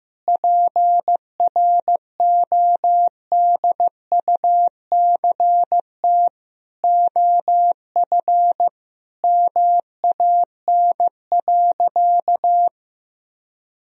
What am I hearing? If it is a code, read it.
Morse code, heard as a keyed tone, about 15 wpm, 700 Hz. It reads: PRODUCT OF MAN.
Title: Sentences from Top 1000 Words - Review Part 2 - 15wpm